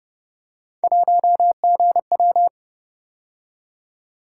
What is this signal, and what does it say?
Morse code, heard as a keyed tone, about 29 wpm, 700 Hz. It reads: 1GW